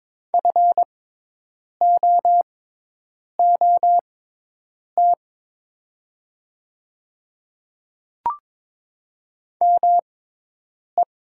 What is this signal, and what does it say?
Morse code, heard as a keyed tone, about 22 wpm, spaced as for 8 wpm, 700 Hz. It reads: FOOT ME
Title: 6x Character Spacing - Top 500 Words - 22wpm